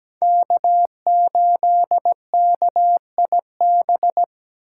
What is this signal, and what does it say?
Morse code, heard as a keyed tone, about 20 wpm, 700 Hz. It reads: K8KIB
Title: Speed Racing - USA 1x3 Call Signs 17wpm